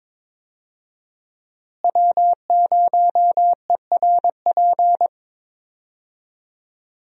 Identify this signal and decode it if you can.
Morse code, heard as a keyed tone, about 22 wpm, 700 Hz. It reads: W0ERP
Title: USA Call Signs - 5 Characters - Repeated 2x - 22wpm